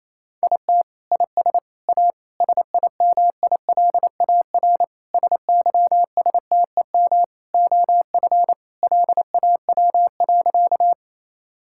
Morse code, heard as a keyed tone, about 28 wpm, 700 Hz. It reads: IT IS A SIMILAR SYSTEM OF LAW.
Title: Sentences from Top 1000 Words - Review Part 5 - 28wpm